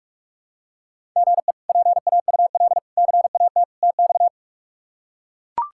Morse code, heard as a keyed tone, about 42 wpm, 700 Hz. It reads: GE PAUL CAT TX
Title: K1USN SST QSO Practice - 45wpm